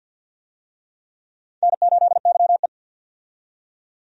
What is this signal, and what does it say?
Morse code, heard as a keyed tone, about 48 wpm, 700 Hz. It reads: N8YE
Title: USA Call Signs - 4 Characters - 50wpm